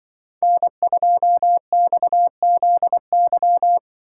Morse code, heard as a keyed tone, about 24 wpm, 700 Hz. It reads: N2XZY
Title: Speed Racing - USA 1x3 Call Signs 20wpm